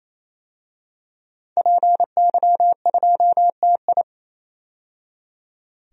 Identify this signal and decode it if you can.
Morse code, heard as a keyed tone, about 28 wpm, 700 Hz. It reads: PY2TI
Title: DXCC-100 Call Signs - 5 Characters - 28wpm